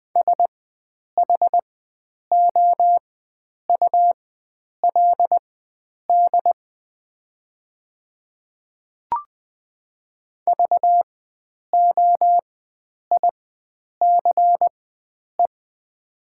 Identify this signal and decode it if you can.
Morse code, heard as a keyed tone, about 21 wpm, spaced as for 9 wpm, 700 Hz. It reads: SHOULD VOICE
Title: 4x Character Spacing - Top 500 Words - 20wpm